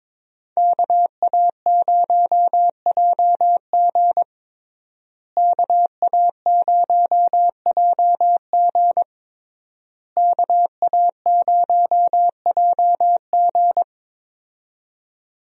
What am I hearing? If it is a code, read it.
Morse code, heard as a keyed tone, about 22 wpm, 700 Hz. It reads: KA0JG KA0JG KA0JG
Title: USA Call Signs - 5 Characters - Repeated 3x - 22wpm